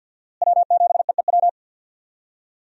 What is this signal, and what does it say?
Morse code, heard as a keyed tone, about 47 wpm, 700 Hz. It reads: W7EEW
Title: USA Call Signs - 5 Characters - Repeated 2x - 50wpm